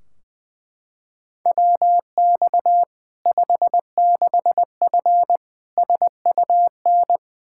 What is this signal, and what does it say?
Morse code, heard as a keyed tone, about 20 wpm, 700 Hz. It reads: WX 56F SUN
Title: Practice QSO - Exchange 2 - Info - 20wpm